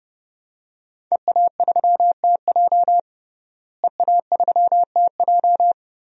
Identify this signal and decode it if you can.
Morse code, heard as a keyed tone, about 30 wpm, 700 Hz. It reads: EA3TJ EA3TJ
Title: DXCC-100 Call Signs - 5 Characters - Repeated 3x - 30wpm